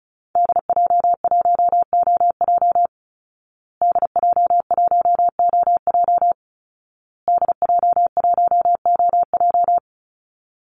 Morse code, heard as a keyed tone, about 34 wpm, 700 Hz. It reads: DJ1OJ DJ1OJ DJ1OJ
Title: Call Signs Repeated 2x - 35wpm